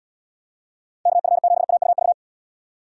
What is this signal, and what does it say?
Morse code, heard as a keyed tone, about 78 wpm, 700 Hz. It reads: KF7AUV